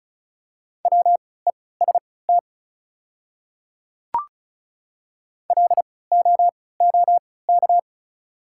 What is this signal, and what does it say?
Morse code, heard as a keyed tone, about 34 wpm, spaced as for 20 wpm, 690 Hz. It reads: WEST LOOK
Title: 3x Character Spacing - Top 500 Words - 35wpm